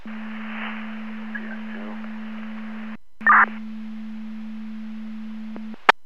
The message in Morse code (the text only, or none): none